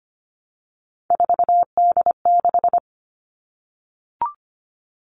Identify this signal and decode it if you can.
Morse code, heard as a keyed tone, about 25 wpm, 700 Hz. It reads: 4D6
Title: All Cut Numbers - 3-Digit 25wpm